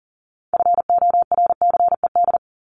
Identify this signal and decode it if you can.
Morse code, heard as a keyed tone, about 40 wpm, 700 Hz. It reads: FORCED
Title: ED Suffix - Top 100 Words with Base Words 40wpm